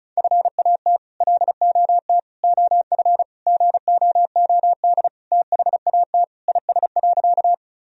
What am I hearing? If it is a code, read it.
Morse code, heard as a keyed tone, about 36 wpm, 700 Hz. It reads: FAT LOT OF GOOD THAT IS.
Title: Sentences from Top 1000 Words - Review Part 4 - 35wpm